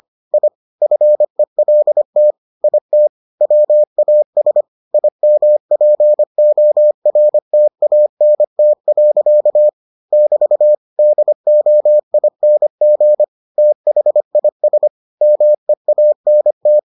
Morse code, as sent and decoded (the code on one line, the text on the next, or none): .. / ..-. .-.. - / .. - / .-- .- ... / .. -- .--. --- .-. - .- -. - .-.-.- / -...- / -.. --- .. -. --. / - .... .. ... / -- . .- -. -
I FELT IT WAS IMPORTANT. <BT> DOING THIS MEANT